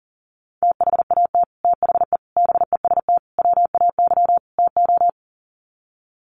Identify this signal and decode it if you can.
Morse code, heard as a keyed tone, about 40 wpm, 700 Hz. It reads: THAT THE BEST WAY TO
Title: Sets of 5 Words Encore2 40wpm